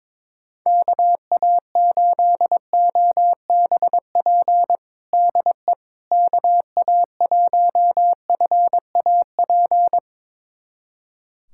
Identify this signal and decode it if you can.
Morse code, heard as a keyed tone, about 22 wpm, 700 Hz. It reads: KA8OBP DE KA1FAP
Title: USA Callsign DE Callsign - 6 Characters - 22wpm